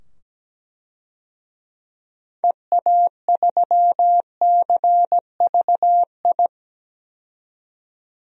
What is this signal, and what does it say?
Morse code, heard as a keyed tone, about 17 wpm, 700 Hz. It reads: EA3CVI
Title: DXCC-100 Call Signs - 6 Characters - 17wpm